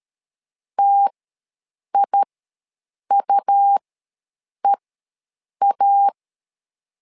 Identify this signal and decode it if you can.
Morse code, heard as a keyed tone, about 13 wpm, spaced as for 7 wpm, 800 Hz. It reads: TIUEA
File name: LEZIONI / 2 UHDN / 2 UHDN ws 12wpm 1708